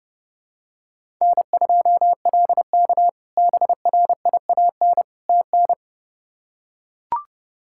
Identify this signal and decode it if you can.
Morse code, heard as a keyed tone, about 30 wpm, 700 Hz. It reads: N2LK BRIAN TN
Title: K1USN SST QSO Practice - 30wpm